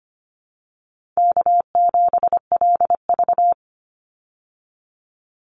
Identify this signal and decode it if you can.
Morse code, heard as a keyed tone, about 25 wpm, 700 Hz. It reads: K7LV
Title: Easy Call Signs Repeated 2x - 25wpm